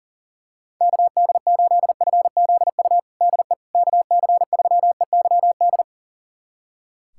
Morse code, heard as a keyed tone, about 40 wpm, 700 Hz. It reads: KD8FZU DE KC3EYD
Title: USA Callsign DE Callsign 40wpm